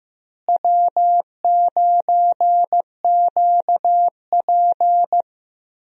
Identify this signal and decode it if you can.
Morse code, heard as a keyed tone, about 15 wpm, 700 Hz. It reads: W9QP
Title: USA Call Signs - 4 Characters - Repeated 3x - 15wpm